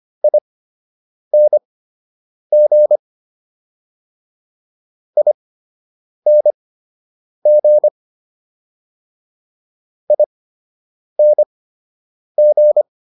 Morse code, heard as a keyed tone, about 25 wpm, spaced as for 8 wpm, 600 Hz. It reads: ING ING ING
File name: ING8 08